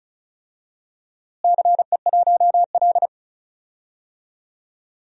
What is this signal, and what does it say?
Morse code, heard as a keyed tone, about 36 wpm, 700 Hz. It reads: CE1L
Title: DXCC-100 Call Signs - 4 Characters - Repeated 2x - 35wpm